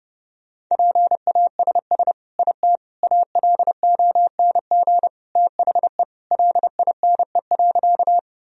Morse code, heard as a keyed tone, about 30 wpm, 700 Hz. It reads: PASS IT ALONG THE LINE.
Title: Sentences from Top 1000 Words - Review Part 1 - 30wpm